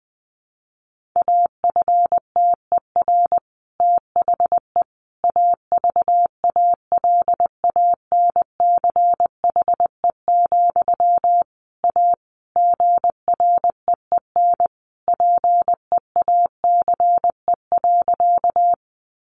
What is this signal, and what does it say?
Morse code, heard as a keyed tone, about 20 wpm, 700 Hz. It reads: AFTER THE AVALANCHE, A GREEN PEACE.
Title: A Sound Of Thunder 20wpm